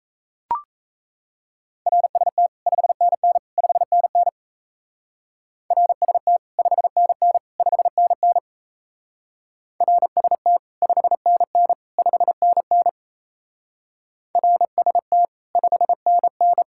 Morse code, heard as a keyed tone, about 34 wpm, 700 Hz. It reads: RST 5NN 5NN RST 5NN 5NN RST 5NN 5NN RST 5NN
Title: Speed-Racing Common QSO Phrases Encore - 28wpm